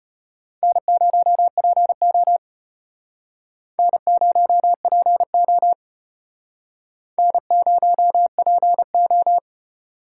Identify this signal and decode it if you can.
Morse code, heard as a keyed tone, about 31 wpm, 700 Hz. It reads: N0PO N0PO N0PO
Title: Speed Racing - USA 1x2 Call Signs 25wpm